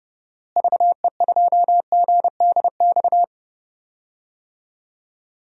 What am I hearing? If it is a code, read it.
Morse code, heard as a keyed tone, about 30 wpm, 700 Hz. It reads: VE2GDX
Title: DXCC-100 Call Signs - 6 Characters - Repeated 3x - 30wpm